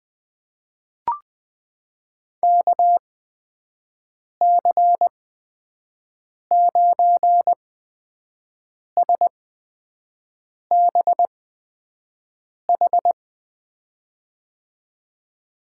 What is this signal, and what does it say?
Morse code, heard as a keyed tone, about 20 wpm, spaced as for 5 wpm, 700 Hz. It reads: KC9SBH
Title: USA Call Signs - 6 Characters - 8x Character Spacing - 20wpm